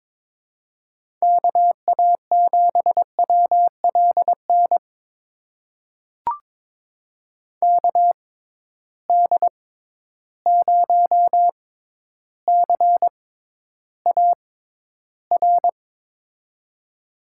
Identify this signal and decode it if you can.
Morse code, heard as a keyed tone, about 22 wpm, spaced as for 11 wpm, 700 Hz. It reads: KA7WLN KD0CAR
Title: USA Call Signs - 6x Character Spacing - 22wpm